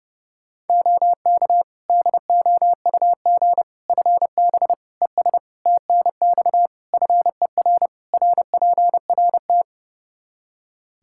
Morse code, heard as a keyed tone, about 30 wpm, 700 Hz. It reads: OK DOUG FB ES TNX FER RPRT
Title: Practice QSO - Exchange 2 - Info - 30wpm